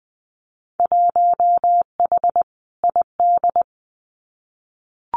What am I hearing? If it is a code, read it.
Morse code, heard as a keyed tone, about 20 wpm, 700 Hz. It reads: 1H ID
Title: Winter Field Day Exchanges 20wpm